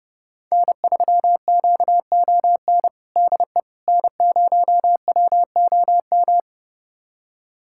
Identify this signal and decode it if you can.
Morse code, heard as a keyed tone, about 32 wpm, 700 Hz. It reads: N3QON DE N0WOM
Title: USA Callsign DE Callsign - 5 Characters - 30wpm